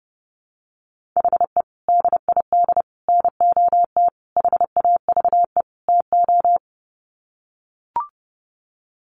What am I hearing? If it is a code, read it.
Morse code, heard as a keyed tone, about 30 wpm, 700 Hz. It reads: HE DID NOT HAVE TO
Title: Sets of 5 Words 30wpm